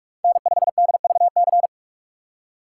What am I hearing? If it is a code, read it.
Morse code, heard as a keyed tone, about 46 wpm, 700 Hz. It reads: N5DVC
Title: USA Call Signs - 5 Characters - 3x Character Spacing - 45wpm